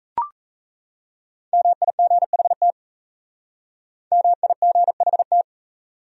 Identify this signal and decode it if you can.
Morse code, heard as a keyed tone, about 37 wpm, 700 Hz. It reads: MIGHT MIGHT